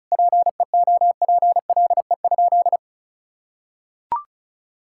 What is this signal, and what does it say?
Morse code, heard as a keyed tone, about 36 wpm, 700 Hz. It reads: PEOPLE?
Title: Sentences from Top 1000 Words - Review Part 5 - 35wpm